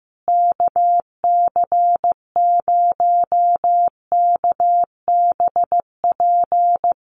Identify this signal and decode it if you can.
Morse code, heard as a keyed tone, about 15 wpm, 700 Hz. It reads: KC0KBP
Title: USA Call Signs Repeated 3x 15wpm